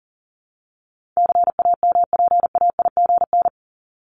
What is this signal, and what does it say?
Morse code, heard as a keyed tone, about 40 wpm, 700 Hz. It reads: CAMPAIGN
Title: Top 1001-1250 Words 40wpm